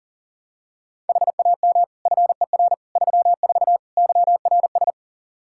Morse code, heard as a keyed tone, about 40 wpm, 700 Hz. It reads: HAM FER 34 YRS
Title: Practice QSO - Exchange 3 - More Info - 40wpm